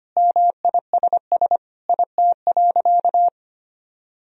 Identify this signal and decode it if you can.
Morse code, heard as a keyed tone, about 25 wpm, 700 Hz. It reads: MISS IT.
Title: Sentences from Top 1000 Words - Review Part 3 - 25wpm